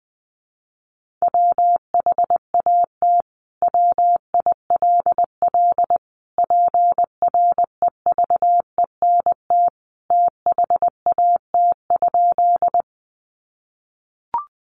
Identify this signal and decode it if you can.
Morse code, heard as a keyed tone, about 20 wpm, 700 Hz. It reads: WHAT WILL PREVENT THAT?
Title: Sentences from Top 1100 Words 20wpm